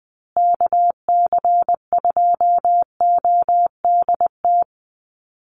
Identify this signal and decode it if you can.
Morse code, heard as a keyed tone, about 20 wpm, 700 Hz. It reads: KC2ODT
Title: USA Call Signs Repeated 2x 20wpm